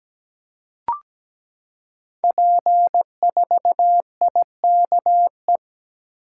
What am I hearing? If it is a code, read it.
Morse code, heard as a keyed tone, about 17 wpm, 700 Hz. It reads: P4IKE
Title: Code Groups - 5 Letter:Numbers - 17wpm